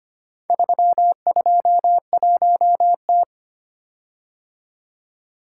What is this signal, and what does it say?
Morse code, heard as a keyed tone, about 25 wpm, 700 Hz. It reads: 321T
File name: practice / 25wpm / Four Letter-Number Combinations - 25wpm